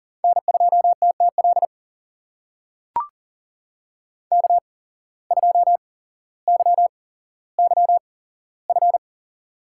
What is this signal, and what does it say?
Morse code, heard as a keyed tone, about 40 wpm, spaced as for 17 wpm, 700 Hz. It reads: N2TTL K2YYF
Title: USA Call Signs - 5 Characters - 8x Character Spacing - 40wpm